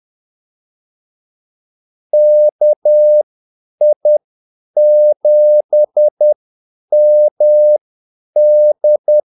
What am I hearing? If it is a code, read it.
Morse code, heard as a keyed tone, about 10 wpm, spaced as for 8 wpm, 600 Hz. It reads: KI7MD
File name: Callsigns-10-8wpm